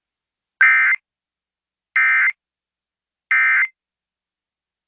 Emergency Alert System (SAME) tones, heard as harsh buzzing bursts. It is heard from 0.6 to 3.7 s.